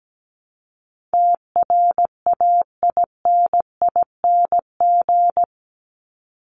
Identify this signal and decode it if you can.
Morse code, heard as a keyed tone, about 17 wpm, 700 Hz. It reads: TRAINING
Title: Top 501-750 Words 17wpm